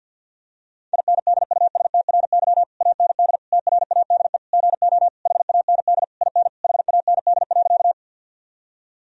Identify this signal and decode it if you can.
Morse code, heard as a keyed tone, about 50 wpm, 700 Hz. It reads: INDUSTRY AND TRADE GO HAND IN HAND.
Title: Sentences from Top 1000 Words - Review Part 4 - 50wpm